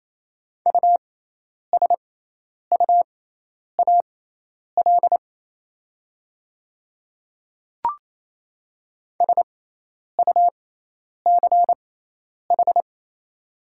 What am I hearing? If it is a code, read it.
Morse code, heard as a keyed tone, about 27 wpm, spaced as for 10 wpm, 700 Hz. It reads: USUAL SUCH
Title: 6x Character Spacing - Top 500 Words - 28wpm